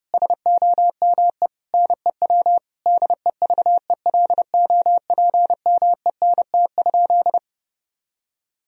Morse code, heard as a keyed tone, about 32 wpm, 700 Hz. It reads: SOME NEW DEVELOPMENT?
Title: Sentences from Top 1000 Words - Review Part 4 - 30wpm